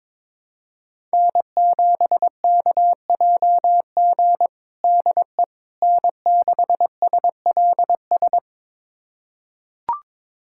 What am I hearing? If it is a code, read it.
Morse code, heard as a keyed tone, about 21 wpm, 700 Hz. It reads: N7KJG DE N6SLS
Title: USA Callsign DE Callsign - 5 Characters - 22wpm